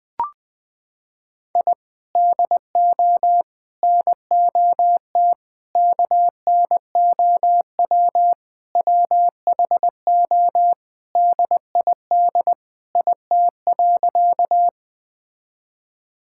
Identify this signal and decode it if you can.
Morse code, heard as a keyed tone, about 21 wpm, 700 Hz. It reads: I DO NOT KNOW WHO DID IT.
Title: Sentences with 5-Word Phrases - 20wpm